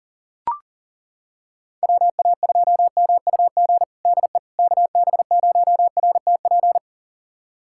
Morse code, heard as a keyed tone, about 40 wpm, 700 Hz. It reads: WA2MUG DE KB0RTP
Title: USA Callsign DE Callsign - 6 Characters - 40wpm